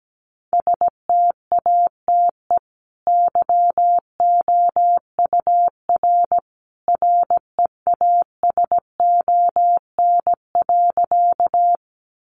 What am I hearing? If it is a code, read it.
Morse code, heard as a keyed tone, about 17 wpm, 700 Hz. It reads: STATE YOUR REASON.
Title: Sentences from Top 700 Words 17wpm